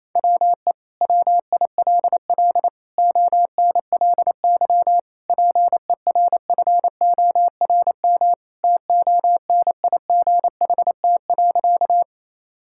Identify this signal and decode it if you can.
Morse code, heard as a keyed tone, about 28 wpm, 700 Hz. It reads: WE WILL ONLY PERFORM TONIGHT.
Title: Sentences from Top 1000 Words - Review Part 5 - 28wpm